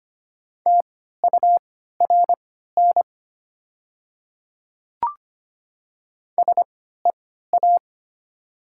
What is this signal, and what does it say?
Morse code, heard as a keyed tone, about 25 wpm, spaced as for 14 wpm, 700 Hz. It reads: TURN SEA
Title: 3x Character Spacing - Top 300 Words - 25wpm